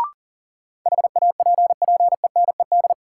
Morse code, heard as a keyed tone, about 40 wpm, 700 Hz. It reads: HAPPENED
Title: Sentences from Top 1000 Words - Review Part 2 - New Word Drill - 40wpm